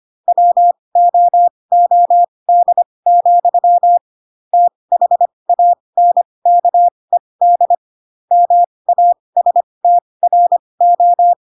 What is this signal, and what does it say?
Morse code, heard as a keyed tone, about 25 wpm, spaced as for 20 wpm, 700 Hz. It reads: WOOD, THANKED MASTRO
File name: Pinocchio / Pinocchio ch2 15 25 ik1hge